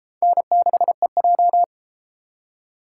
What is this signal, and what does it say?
Morse code, heard as a keyed tone, about 32 wpm, 700 Hz. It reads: N6EJ